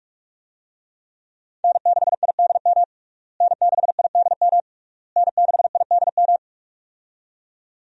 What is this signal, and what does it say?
Morse code, heard as a keyed tone, about 46 wpm, 700 Hz. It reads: N6IDM N6IDM N6IDM
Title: USA Call Signs - 5 Characters - Repeated 3x - 45wpm